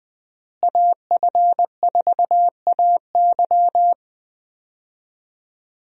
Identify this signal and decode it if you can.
Morse code, heard as a keyed tone, about 20 wpm, 700 Hz. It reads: AF4AY
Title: USA Call Signs - 5 Characters - Repeated 3x - 20wpm